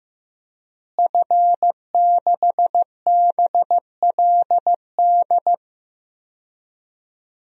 Morse code, heard as a keyed tone, about 15 wpm, 700 Hz. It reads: F6BLD
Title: DXCC-100 Call Signs - 5 Characters - Repeated 2x - 15wpm